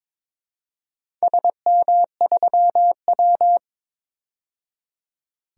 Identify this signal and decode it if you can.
Morse code, heard as a keyed tone, about 22 wpm, 700 Hz. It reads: SM3W
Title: DXCC-100 Call Signs - 4 Characters - 22wpm